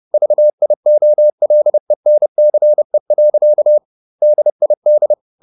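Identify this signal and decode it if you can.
Morse code, heard as a keyed tone, about 30 wpm, 600 Hz. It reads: VIOLENCE. DID
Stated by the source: Morse code at 30 wpm, the novel Never by Ken Follett in English, from CW med Gnister.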